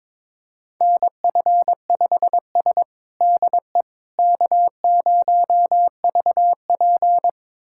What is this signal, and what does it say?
Morse code, heard as a keyed tone, about 22 wpm, 700 Hz. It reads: NF5S DE K0VP
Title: USA Callsign DE Callsign - 4 Characters - 22wpm